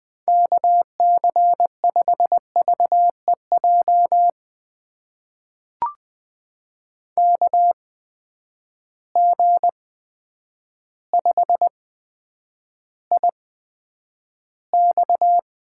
Morse code, heard as a keyed tone, about 20 wpm, spaced as for 9 wpm, 700 Hz. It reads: KC5VEJ KG5IX
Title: USA Call Signs - 6 Characters - 8x Character Spacing - 20wpm